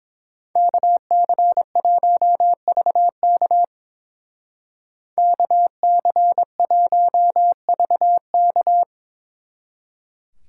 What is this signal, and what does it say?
Morse code, heard as a keyed tone, about 27 wpm, 700 Hz. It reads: KC1VK KC1VK
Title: Speed Racing - USA 2x2 Call Signs 22wpm